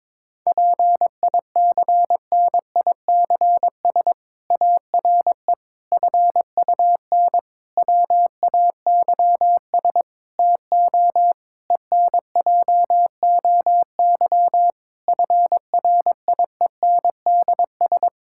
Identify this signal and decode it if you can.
Morse code, heard as a keyed tone, about 22 wpm, 700 Hz. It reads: PICNICS ARE FUN WAYS TO ENJOY FRIENDS
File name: Difficult Sentences with Hints - 22wpm